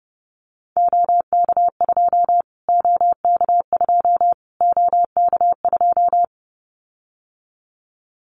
Morse code, heard as a keyed tone, about 30 wpm, 700 Hz. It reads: OK2 OK2 OK2
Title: Callsign Prefixes for Top 25 DXCC - Repeated 3x - 30wpm